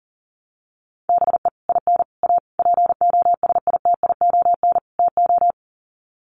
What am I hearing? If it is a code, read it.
Morse code, heard as a keyed tone, about 40 wpm, 700 Hz. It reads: BE IN A POSITION TO
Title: Sets of 5 Words Encore2 40wpm